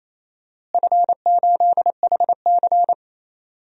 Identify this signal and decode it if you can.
Morse code, heard as a keyed tone, about 28 wpm, 700 Hz. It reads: F8HC
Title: DXCC-100 Call Signs - 4 Characters - Repeated 2x - 28wpm